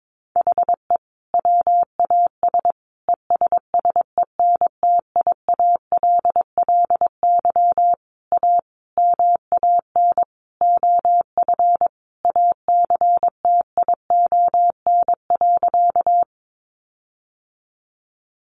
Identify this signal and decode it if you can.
Morse code, heard as a keyed tone, about 22 wpm, 700 Hz. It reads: HE WAS ESSENTIALLY A MAN OF ACTION.
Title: Sentences from Top 1300 Words 22wpm